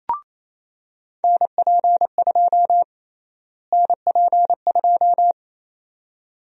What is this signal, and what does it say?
Morse code, heard as a keyed tone, about 28 wpm, 700 Hz. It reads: NP2 NP2
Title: USA Call Sign Prefixes - Repeated 2x - 28wpm